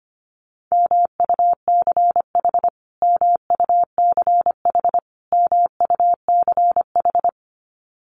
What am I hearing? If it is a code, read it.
Morse code, heard as a keyed tone, about 25 wpm, 700 Hz. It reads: MUCH MUCH MUCH
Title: Top 300 Words Repeated 25wpm